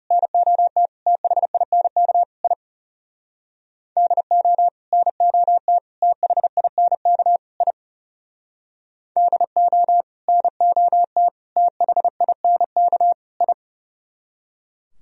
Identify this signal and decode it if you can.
Morse code, heard as a keyed tone, about 36 wpm, 700 Hz. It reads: NOT THINK I DO NOT THINK I DO NOT THINK I
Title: Speed-Racing - Sets of 4 Words 30wpm